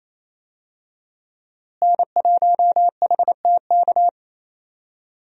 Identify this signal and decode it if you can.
Morse code, heard as a keyed tone, about 28 wpm, 700 Hz. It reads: N1HTK
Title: USA Call Signs - 5 Characters - Repeated 3x - 28wpm